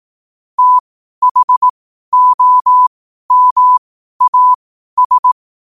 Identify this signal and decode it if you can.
Morse code, heard as a keyed tone, about 18 wpm, spaced as for 12 wpm, 1000 Hz. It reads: THOMAS
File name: qotdfiles / 12wpm